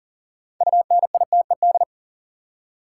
Morse code, heard as a keyed tone, about 40 wpm, 700 Hz. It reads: UNITED
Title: Six Letter Words - 40wpm